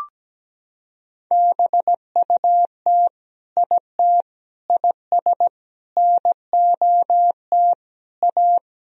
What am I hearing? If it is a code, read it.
Morse code, heard as a keyed tone, about 17 wpm, 700 Hz. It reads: BUT IT IS NOT A
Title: Sentences from Top 1000 Words - Review Part 1 - 17wpm